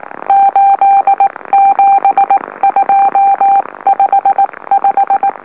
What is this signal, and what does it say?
Morse code, heard as a keyed tone, about 19 wpm, spaced as for 16 wpm, 790 Hz. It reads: 87255
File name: M04- -2345utc-LOLO